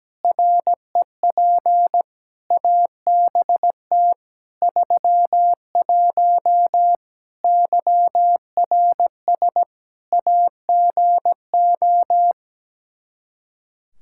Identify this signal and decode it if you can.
Morse code, heard as a keyed tone, about 17 wpm, 700 Hz. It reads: REP ABT 31 YRS AGO
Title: Practice QSO - Exchange 3 - More Info - 17wpm